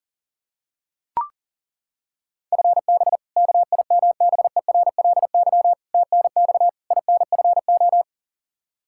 Morse code, heard as a keyed tone, about 40 wpm, 700 Hz. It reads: FB KIMBERLY TNX INFO
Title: Practice QSO - Exchange 3 - More Info - 40wpm